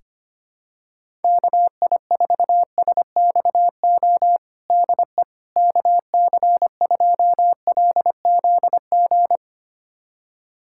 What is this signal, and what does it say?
Morse code, heard as a keyed tone, about 25 wpm, 700 Hz. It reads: KI4SXO DE KC2LZG
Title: USA Callsign DE Callsign 25wpm